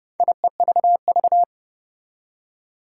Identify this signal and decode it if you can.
Morse code, heard as a keyed tone, about 30 wpm, 700 Hz. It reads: IEVV
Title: Four Challenging Letter-Number Combinations 30wpm